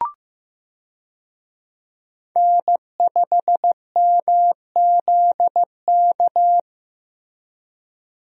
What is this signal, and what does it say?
Morse code, heard as a keyed tone, about 16 wpm, 700 Hz. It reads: N5MZK
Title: USA Call Signs - 5 Characters - Repeated 2x - 15wpm